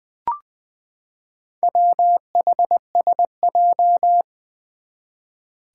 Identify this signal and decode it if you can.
Morse code, heard as a keyed tone, about 20 wpm, 700 Hz. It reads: WHSJ
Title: Four Challenging Letter-Number Combinations 20wpm